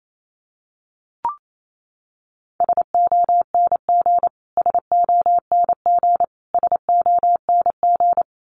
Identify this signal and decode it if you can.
Morse code, heard as a keyed tone, about 28 wpm, 700 Hz. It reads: SONG SONG SONG
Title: Top 500 Words Repeated 28wpm